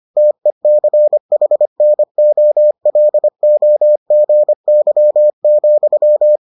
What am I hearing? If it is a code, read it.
Morse code, heard as a keyed tone, about 25 wpm, 600 Hz. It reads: TECHNOLOGY,